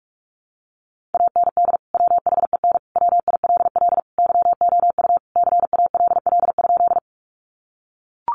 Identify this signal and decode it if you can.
Morse code, heard as a keyed tone, about 44 wpm, 700 Hz. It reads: AND WHEN WILL YOU CALL?
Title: Sentences from Top 100 Words 45wpm